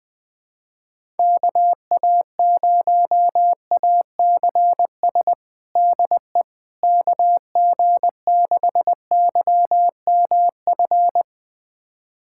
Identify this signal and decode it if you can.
Morse code, heard as a keyed tone, about 20 wpm, 700 Hz. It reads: KA0ACS DE KG6YMF
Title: USA Callsign DE Callsign 20wpm